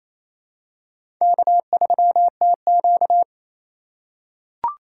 Morse code, heard as a keyed tone, about 28 wpm, 700 Hz. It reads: K3TQ
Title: USA Call Signs - 4 Characters - 4x Character Spacing - 28wpm